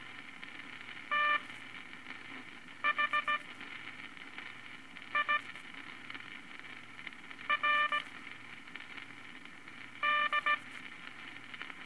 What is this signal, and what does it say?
Morse code, heard as a keyed tone, about 17 wpm, spaced as for 4 wpm, 1180 Hz, with noise. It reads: THIRD
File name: Morse Code Records/15A LCM 158 Units I-V Groups of Five